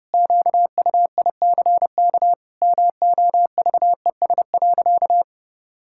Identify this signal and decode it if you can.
Morse code, heard as a keyed tone, about 30 wpm, 700 Hz. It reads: QUICK MOVES.
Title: Pangram Challenge v2 - 30wpm